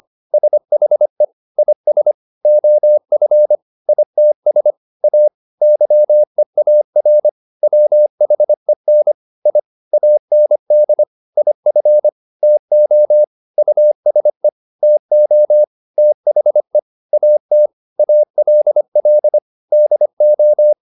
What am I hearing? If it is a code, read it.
Morse code, heard as a keyed tone, about 25 wpm, 600 Hz. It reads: SHE IS OF ITS A YEAR WHEN I AND IF TO USE TO THE AT ALL DO